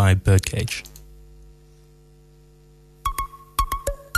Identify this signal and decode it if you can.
Dial tone (UK), 350 + 450 Hz, with speech.